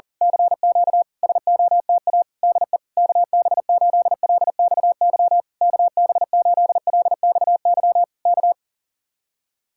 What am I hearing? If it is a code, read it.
Morse code, heard as a keyed tone, about 37 wpm, 700 Hz. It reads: CQ SOTA DE KB8LXY KB8LXY K
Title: Speed-Racing Common QSO Phrases Encore - 40wpm